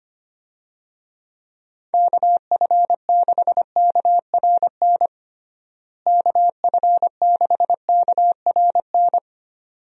Morse code, heard as a keyed tone, about 25 wpm, 700 Hz. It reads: KF6KRN KF6KRN
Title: USA Call Signs - 6 Characters - Repeated 2x - 25wpm